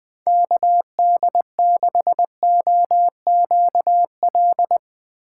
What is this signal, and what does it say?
Morse code, heard as a keyed tone, about 20 wpm, 700 Hz. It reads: KD6OQL
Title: USA Call Signs - 2x Character Spacing - 20wpm